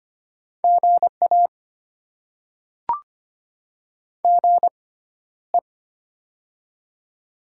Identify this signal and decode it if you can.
Morse code, heard as a keyed tone, about 25 wpm, spaced as for 13 wpm, 700 Hz. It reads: GA GE
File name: Top 20 QSO Elements - 6x Character Spacing - 25wpm